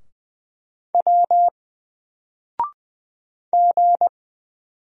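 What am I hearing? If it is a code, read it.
Morse code, heard as a keyed tone, about 20 wpm, 700 Hz. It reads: W G
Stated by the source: Single letter and number Morse drill at 20 wpm, from N4LCD.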